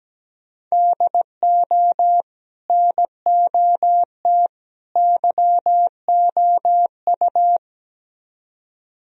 Morse code, heard as a keyed tone, about 18 wpm, 700 Hz. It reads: DO NOT YOU
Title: Speed-Racing - Sets of 3 Words 17wpm